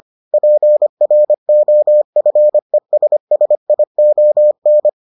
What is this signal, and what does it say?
Morse code, heard as a keyed tone, about 25 wpm, 600 Hz. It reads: PROFESSION